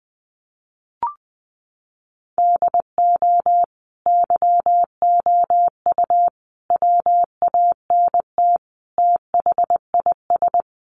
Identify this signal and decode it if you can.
Morse code, heard as a keyed tone, about 20 wpm, 700 Hz. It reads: DO YOU WANT THIS
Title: Sentences from Top 1500 Words 20wpm